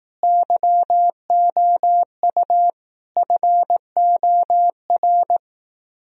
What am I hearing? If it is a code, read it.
Morse code, heard as a keyed tone, about 18 wpm, 700 Hz. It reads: YOU FOR
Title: Speed-Racing - Sets of 3 Words 15wpm